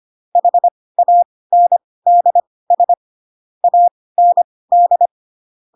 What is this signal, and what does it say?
Morse code, heard as a keyed tone, about 25 wpm, spaced as for 18 wpm, 700 Hz. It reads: HANDS AND
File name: Pinocchio / Pinocchio ch2 12 25 ik1hge